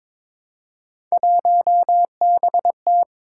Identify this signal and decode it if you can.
Morse code, heard as a keyed tone, about 22 wpm, 700 Hz. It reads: 1BT
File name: practice / 22wpm / Three Letter-Number Combinations - 22wpm